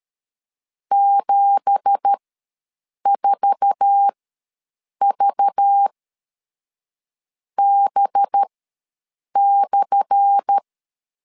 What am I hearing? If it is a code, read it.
Morse code, heard as a keyed tone, about 13 wpm, spaced as for 6 wpm, 800 Hz. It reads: 74VB/
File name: LEZIONI / 10 SIGNS / 10A SLASH ws 12wpm 3443